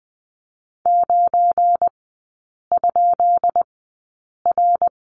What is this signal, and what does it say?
Morse code, heard as a keyed tone, about 20 wpm, spaced as for 8 wpm, 700 Hz. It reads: 9?R
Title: 172 Lesson 28e - Number 9 - Three Characters 20wpm